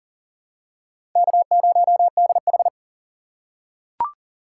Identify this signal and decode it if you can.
Morse code, heard as a keyed tone, about 40 wpm, 700 Hz. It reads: K0DH
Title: USA Call Signs - 2x Character Spacing - 40wpm